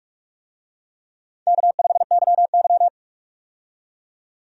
Morse code, heard as a keyed tone, about 41 wpm, 700 Hz. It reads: K5YY
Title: USA Call Signs - 4 Characters - 45wpm